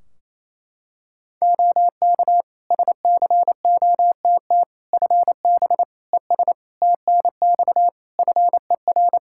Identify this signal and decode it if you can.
Morse code, heard as a keyed tone, about 28 wpm, 700 Hz. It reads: OK SCOTT FB ES TNX FER
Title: Practice QSO - Exchange 2 - Info - 28wpm